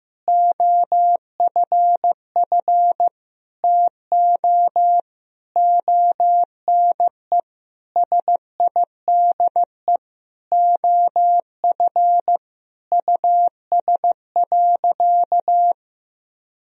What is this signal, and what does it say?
Morse code, heard as a keyed tone, about 15 wpm, 700 Hz. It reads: OFF TO ONE SIDE OF US.
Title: Sentences from Top 1000 Words - Review Part 3 - 15wpm